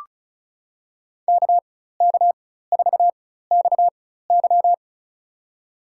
Morse code, heard as a keyed tone, about 34 wpm, spaced as for 16 wpm, 700 Hz. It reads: KK4XY
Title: USA Call Signs - 5 Characters - 4x Character Spacing - 35wpm